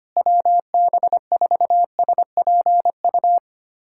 Morse code, heard as a keyed tone, about 25 wpm, 700 Hz. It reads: WB4SPU
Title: USA Callsign DE Callsign - 6 Characters - 25wpm